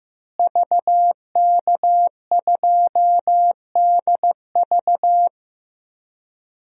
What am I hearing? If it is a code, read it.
Morse code, heard as a keyed tone, about 16 wpm, 700 Hz. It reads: VK2DV
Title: DXCC-100 Call Signs - 5 Characters - Repeated 2x - 15wpm